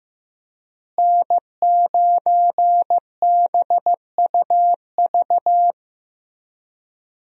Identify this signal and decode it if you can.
Morse code, heard as a keyed tone, about 15 wpm, 700 Hz. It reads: N9BUV
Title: USA Call Signs - 5 Characters - Repeated 3x - 15wpm